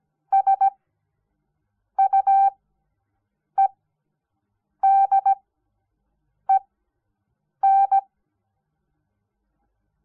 Morse code, heard as a keyed tone, about 17 wpm, spaced as for 6 wpm, 780 Hz. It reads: SUEDEN
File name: Onlinelehrgang / morsen / DJ4UF Morsen06